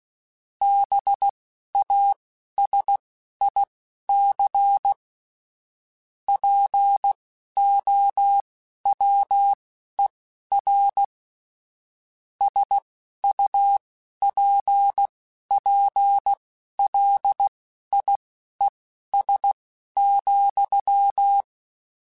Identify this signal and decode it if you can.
Morse code, heard as a keyed tone, about 16 wpm, spaced as for 11 wpm, 800 Hz. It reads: BASIC POWER SUPPLIES,
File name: Technical Subjects / Technical Subjects-mp3s / Sw Pwr Sup @10